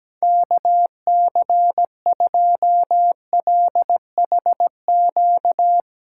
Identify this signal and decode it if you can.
Morse code, heard as a keyed tone, about 17 wpm, 700 Hz. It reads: KC2LHQ
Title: USA Call Signs - 6 Characters - 8x Character Spacing - 17wpm